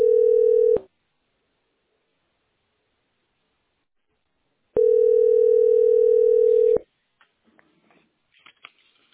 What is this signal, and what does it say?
Ringback tone (North America), 440 + 480 Hz.
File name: 092cdd33-09ec-4146-8ac7- a7d